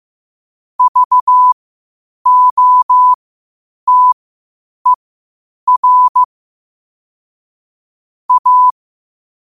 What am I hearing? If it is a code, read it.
Morse code, heard as a keyed tone, about 15 wpm, spaced as for 8 wpm, 1000 Hz. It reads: VOTER A
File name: qotdfiles / 8wpm